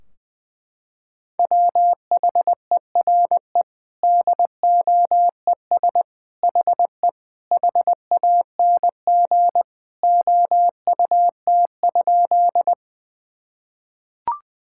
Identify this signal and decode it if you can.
Morse code, heard as a keyed tone, about 20 wpm, 700 Hz. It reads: WHERE DOES HE HANG OUT?
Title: Sentences from Top 1600 Words 20wpm